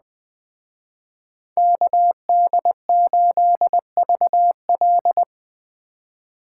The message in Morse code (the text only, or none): KD8VL